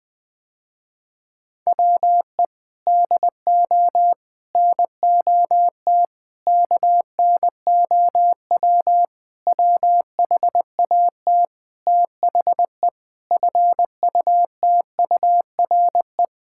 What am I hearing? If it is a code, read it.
Morse code, heard as a keyed tone, about 20 wpm, 700 Hz. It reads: WE DO NOT KNOW WHAT THE FUTURE
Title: Sentences with 5-Word Phrases - Spoken Beginning - 20wpm